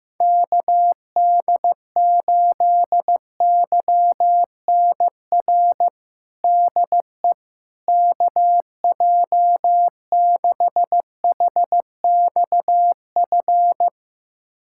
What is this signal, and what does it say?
Morse code, heard as a keyed tone, about 15 wpm, 700 Hz. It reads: KD8YNR DE KJ6HXF
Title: USA Callsign DE Callsign 15wpm